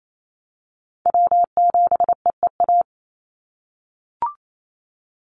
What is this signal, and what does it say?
Morse code, heard as a keyed tone, about 28 wpm, 700 Hz. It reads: W7EEA
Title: USA Call Signs 28wpm